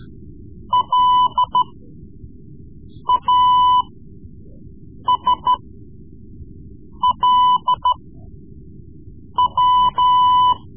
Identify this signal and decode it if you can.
Morse code, heard as a keyed tone, about 13 wpm, spaced as for 5 wpm, 1000 Hz. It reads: LASLW
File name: Morse Lessons / Morse Lessons / Tape-3B